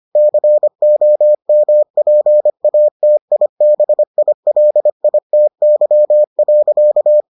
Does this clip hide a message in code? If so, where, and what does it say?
Yes: Morse code from 0.2 to 7.2 s, heard as a keyed tone, about 25 wpm, 600 Hz. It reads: COMPATIBILITY.